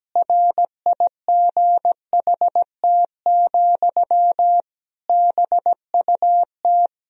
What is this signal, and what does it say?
Morse code, heard as a keyed tone, about 17 wpm, 700 Hz. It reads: RIGHT, BUT
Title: Sentences from Top 1000 Words - Review Part 2 - 17wpm